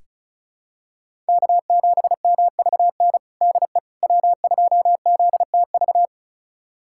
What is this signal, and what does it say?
Morse code, heard as a keyed tone, about 36 wpm, 700 Hz. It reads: K7MVN DE W2ZTV
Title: USA Callsign DE Callsign 35wpm